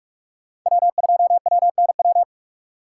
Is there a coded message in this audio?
Yes: Morse code from 0.7 to 2.2 s, heard as a keyed tone, about 58 wpm, 700 Hz. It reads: W2WNW